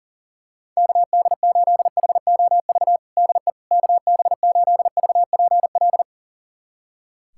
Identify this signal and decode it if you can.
Morse code, heard as a keyed tone, about 40 wpm, 700 Hz. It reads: KD8HOV DE KB8VPL
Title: USA Callsign DE Callsign - 6 Characters - 40wpm